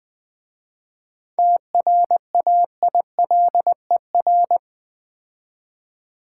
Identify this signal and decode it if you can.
Morse code, heard as a keyed tone, about 20 wpm, 700 Hz. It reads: TRAILER
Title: 7 Letter Words - First 3 Letters Given - 20wpm